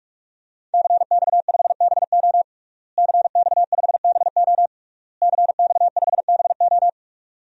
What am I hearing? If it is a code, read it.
Morse code, heard as a keyed tone, about 41 wpm, 700 Hz. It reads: CX5BO CX5BO CX5BO
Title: DXCC-100 Call Signs - 5 Characters - Repeated 3x - 45wpm